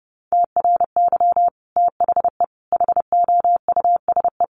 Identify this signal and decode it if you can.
Morse code, heard as a keyed tone, about 28 wpm, 700 Hz. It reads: TRY THE HOUSE
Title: Sentences from Top 1300 Words 30wpm